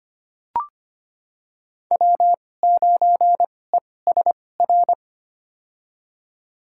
Morse code, heard as a keyed tone, about 25 wpm, spaced as for 18 wpm, 700 Hz. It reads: W9ESR
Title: USA Call Signs - 5 Characters - 2x Character Spacing - 25wpm